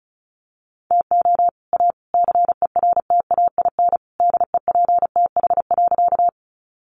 Morse code, heard as a keyed tone, about 34 wpm, 700 Hz. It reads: TO A CERTAIN DEPTH.
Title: Sentences from Top 1200 Words 35wpm